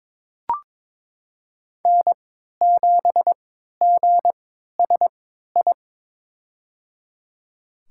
Morse code, heard as a keyed tone, about 21 wpm, spaced as for 12 wpm, 700 Hz. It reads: N7GSI
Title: USA Call Signs - 5 Characters - 3x Character Spacing - 22wpm